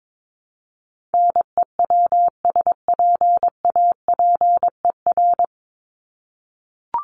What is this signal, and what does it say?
Morse code, heard as a keyed tone, about 22 wpm, 700 Hz. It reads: NEWSPAPER
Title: Top 751-1000 Words 22wpm